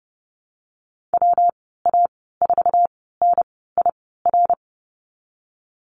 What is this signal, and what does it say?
Morse code, heard as a keyed tone, about 30 wpm, 700 Hz. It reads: W A 4 N I R
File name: Lesson 19g - Letter w - Call Signs 30wpm